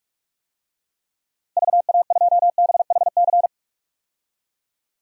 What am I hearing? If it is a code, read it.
Morse code, heard as a keyed tone, about 46 wpm, 700 Hz. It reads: VA2BHC
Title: DXCC-100 Call Signs - 6 Characters - Repeated 2x - 45wpm